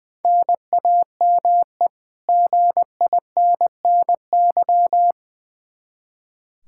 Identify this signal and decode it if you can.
Morse code, heard as a keyed tone, about 20 wpm, 700 Hz. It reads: NAME GINNY
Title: Practice QSO - Exchange 1 - Beginning - 20wpm